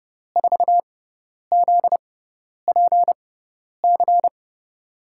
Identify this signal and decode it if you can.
Morse code, heard as a keyed tone, about 30 wpm, spaced as for 10 wpm, 700 Hz. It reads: 4ZPC
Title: USA Call Signs - 6 Characters - 6x Character Spacing - 30wpm